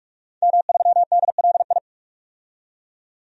Morse code, heard as a keyed tone, about 46 wpm, 700 Hz. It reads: M3DLI
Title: DXCC-100 Call Signs - 5 Characters - 45wpm